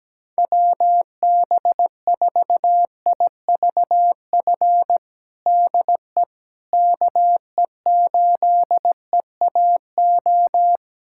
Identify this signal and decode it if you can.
Morse code, heard as a keyed tone, about 17 wpm, 700 Hz. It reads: WB4IVF DE KE8EAO
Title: USA Callsign DE Callsign 17wpm